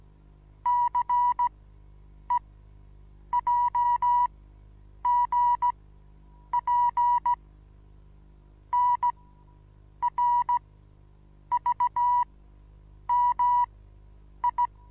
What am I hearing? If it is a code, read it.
Morse code, heard as a keyed tone, about 16 wpm, spaced as for 8 wpm, 970 Hz. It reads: CEJGP NRVMI
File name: Onlinelehrgang / morsen / DJ4UF Morsen25